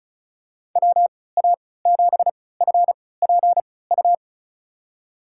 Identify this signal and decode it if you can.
Morse code, heard as a keyed tone, about 34 wpm, spaced as for 20 wpm, 700 Hz. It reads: WA7FPU
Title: USA Call Signs - 6 Characters - 3x Character Spacing - 35wpm